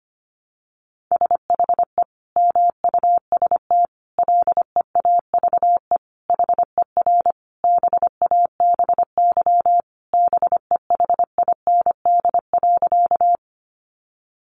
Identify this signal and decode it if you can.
Morse code, heard as a keyed tone, about 25 wpm, 700 Hz. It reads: SHE MUST LEAVE HER BABY BEHIND.
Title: Sentences from Top 1300 Words 25wpm